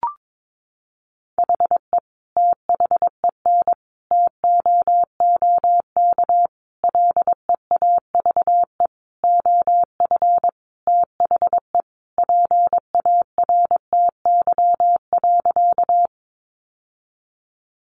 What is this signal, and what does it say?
Morse code, heard as a keyed tone, about 22 wpm, 700 Hz. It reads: HE THEN TOOK LEAVE OF THE PARTY.
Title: Sentences from Top 1400 Words 22wpm